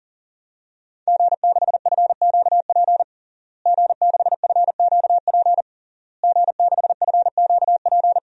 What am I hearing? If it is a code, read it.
Morse code, heard as a keyed tone, about 40 wpm, 700 Hz. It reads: G6FQP G6FQP G6FQP
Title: DXCC-100 Call Signs - 5 Characters - Repeated 3x - 40wpm